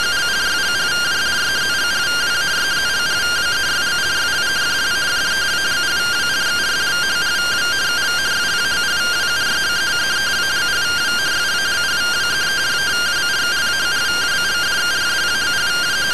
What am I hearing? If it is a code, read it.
Two-tone teleprinter signal, about 170 Hz shift.